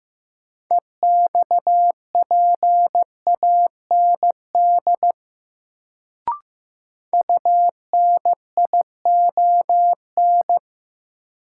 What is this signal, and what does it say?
Morse code, heard as a keyed tone, about 15 wpm, 700 Hz. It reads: EXPAND UNION
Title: Sentences from Top 1400 Words - New Word Drill Alt - 15wpm